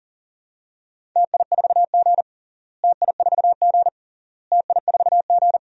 Morse code, heard as a keyed tone, about 40 wpm, 700 Hz. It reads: TI4G TI4G TI4G
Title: DXCC-100 Call Signs - 4 Characters - Repeated 3x - 40wpm